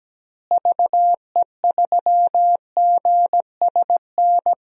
Morse code, heard as a keyed tone, about 17 wpm, 700 Hz. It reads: VE3GSN
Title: DXCC-100 Call Signs - 6 Characters - Repeated 2x - 17wpm